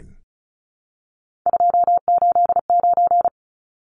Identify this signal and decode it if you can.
Morse code, heard as a keyed tone, about 34 wpm, 700 Hz. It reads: 289
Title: Numbers - 3-Digit 35wpm